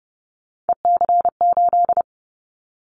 Morse code, heard as a keyed tone, about 30 wpm, 700 Hz. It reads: EC8
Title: Callsign Prefixes for Top 100 DXCC - 30wpm